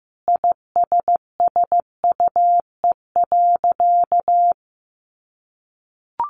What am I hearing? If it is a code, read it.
Morse code, heard as a keyed tone, about 15 wpm, 700 Hz. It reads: ISSUE.
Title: Sentences from Top 900 Words 15wpm